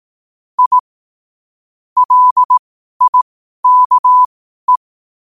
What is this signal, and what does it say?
Morse code, heard as a keyed tone, about 18 wpm, spaced as for 12 wpm, 1000 Hz. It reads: I LIKE
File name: qotdfiles / 12wpm